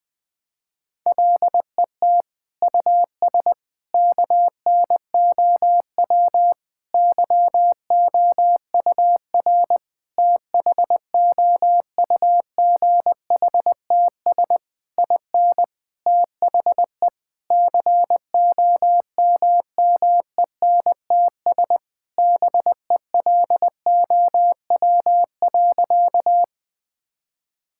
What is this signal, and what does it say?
Morse code, heard as a keyed tone, about 20 wpm, 700 Hz. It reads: LET US KNOW YOUR THOUGHTS IN THE COMMENTS BELOW.
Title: Sentences with 9-Word Phrases - Spoken Beginning - 20wpm